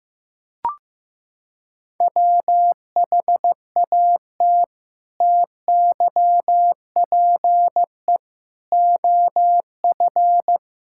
Morse code, heard as a keyed tone, about 15 wpm, 700 Hz. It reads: WHAT TYPE OF
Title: Sentences from Top 1000 Words - Review Part 5 - 15wpm